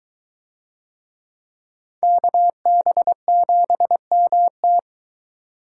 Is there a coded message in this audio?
Yes: Morse code from 2.0 to 4.8 s, heard as a keyed tone, about 22 wpm, 700 Hz. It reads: KB7MT